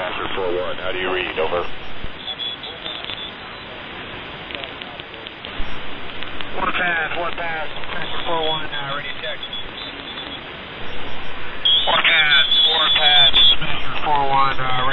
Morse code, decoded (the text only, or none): none